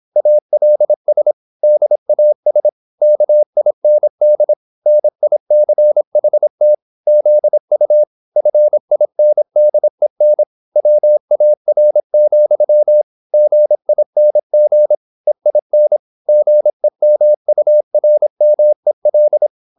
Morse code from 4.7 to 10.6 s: NICHT ZU FINDEN